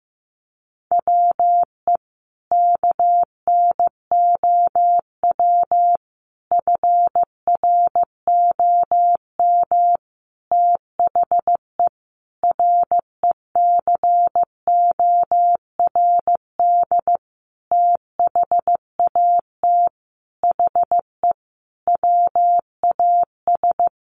Morse code, heard as a keyed tone, about 14 wpm, 700 Hz. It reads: WE KNOW FROM THE RECORD THAT HE WAS
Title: Sentences from Top 500 Words 15wpm